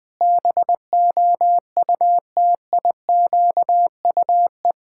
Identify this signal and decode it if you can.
Morse code, heard as a keyed tone, about 21 wpm, 700 Hz. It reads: BOUTIQUE
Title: Pangram Challenge v2 - 20wpm